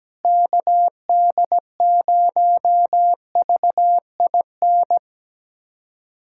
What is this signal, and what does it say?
Morse code, heard as a keyed tone, about 17 wpm, 700 Hz. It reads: KD0VIN
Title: USA Callsign DE Callsign 17wpm